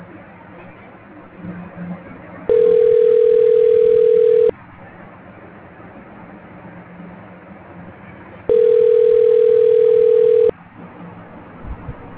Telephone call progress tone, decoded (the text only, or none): ringback tone